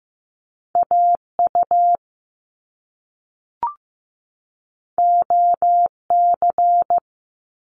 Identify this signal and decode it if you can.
Morse code, heard as a keyed tone, about 15 wpm, 700 Hz. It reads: AU OC
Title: Top Two Letter Combinations in Words 15wpm